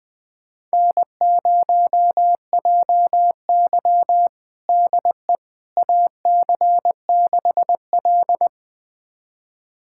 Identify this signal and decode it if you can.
Morse code, heard as a keyed tone, about 20 wpm, 700 Hz. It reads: N0JY DE AC6L
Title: USA Callsign DE Callsign - 4 Characters - 20wpm